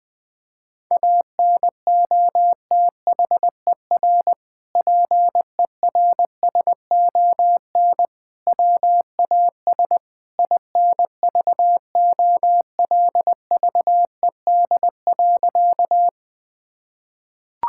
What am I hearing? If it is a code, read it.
Morse code, heard as a keyed tone, about 21 wpm, 700 Hz. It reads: ANOTHER PERSON WAS INVOLVED.
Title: Sentences from Top 1000 Words - Review Part 5 - 20wpm